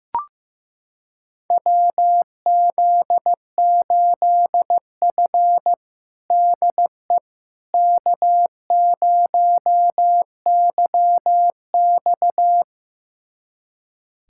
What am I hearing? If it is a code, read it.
Morse code, heard as a keyed tone, about 15 wpm, 700 Hz. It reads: WZ8F DE K0YX
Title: USA Callsign DE Callsign - 4 Characters - 15wpm